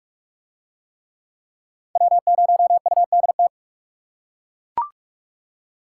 Morse code, heard as a keyed tone, about 40 wpm, 700 Hz. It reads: W0UDT